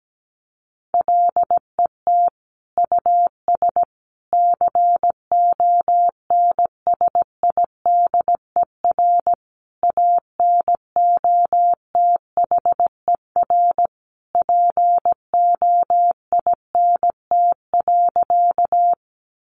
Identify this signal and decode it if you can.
Morse code, heard as a keyed tone, about 17 wpm, 700 Hz. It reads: LET US CONSIDER ANOTHER POINT.
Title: Sentences from Top 600 Words 17wpm